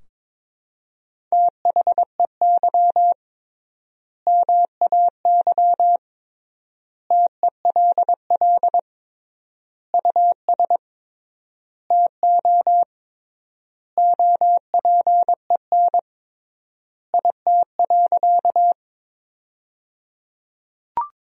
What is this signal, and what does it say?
Morse code, heard as a keyed tone, about 22 wpm, 700 Hz. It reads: THEY MAY TELL US TO OPEN IT.
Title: 3x Word-Spacing - Sentences from Top 300 Words - 22wpm